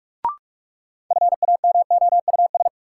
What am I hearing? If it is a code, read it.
Morse code, heard as a keyed tone, about 45 wpm, 700 Hz. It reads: FAMOUS
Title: Sentences from Top 1500 Words - New Word Drill Alt - 45wpm